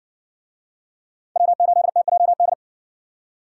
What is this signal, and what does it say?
Morse code, heard as a keyed tone, about 51 wpm, 700 Hz. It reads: W8TJD